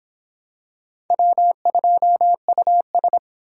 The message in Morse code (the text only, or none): W2US